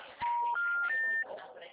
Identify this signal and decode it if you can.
Special information tone, with speech.